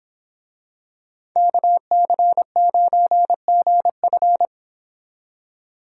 Morse code, heard as a keyed tone, about 26 wpm, 700 Hz. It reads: KC9GF
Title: Speed Racing - USA 2x2 Call Signs 22wpm